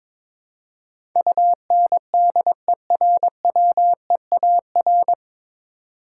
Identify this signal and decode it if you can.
Morse code, heard as a keyed tone, about 22 wpm, 700 Hz. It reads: UNDERWEAR
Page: Morse Code Ninja practice 22wpm